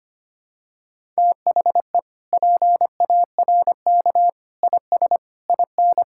Morse code, heard as a keyed tone, about 25 wpm, 700 Hz. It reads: THE PARK IS IN
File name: practice / 25wpm / Sentences with 4-Word Phrases - 25wpm